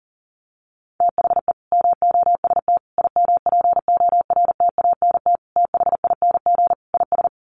Morse code, heard as a keyed tone, about 37 wpm, 700 Hz. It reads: THE MOST IMPORTANT THING IS
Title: Sets of 5 Words Encore 40wpm